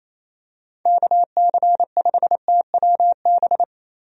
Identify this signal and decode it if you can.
Morse code, heard as a keyed tone, about 27 wpm, 700 Hz. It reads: KC5TWB
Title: USA Call Signs - 6 Characters - Repeated 2x - 28wpm